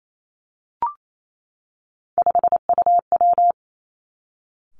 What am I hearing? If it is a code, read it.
Morse code, heard as a keyed tone, about 28 wpm, 700 Hz. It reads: 5UW
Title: Three Challenging Letter-Number Combinations 28wpm